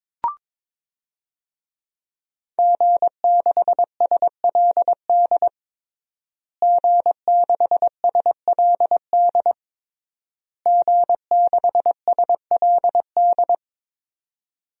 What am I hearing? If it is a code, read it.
Morse code, heard as a keyed tone, about 22 wpm, 700 Hz. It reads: G6SLD G6SLD G6SLD
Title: DXCC-100 Call Signs - 5 Characters - Repeated 3x - 22wpm